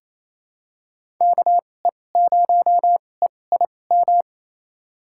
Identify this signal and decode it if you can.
Morse code, heard as a keyed tone, about 29 wpm, spaced as for 20 wpm, 700 Hz. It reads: KE0EIM
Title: USA Call Signs - 6 Characters - 2x Character Spacing - 28wpm